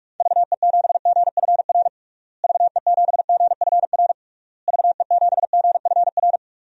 Morse code, heard as a keyed tone, about 41 wpm, 700 Hz. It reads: VE7GFR VE7GFR VE7GFR
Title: DXCC-100 Call Signs - 6 Characters - Repeated 3x - 45wpm